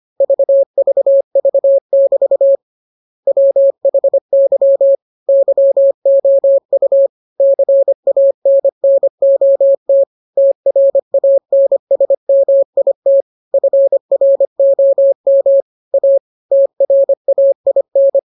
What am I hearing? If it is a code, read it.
Morse code, heard as a keyed tone, about 25 wpm, 550 Hz. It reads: VVV<BT> WHY YOU CANNOT TRANSMIT FROM A TRAIN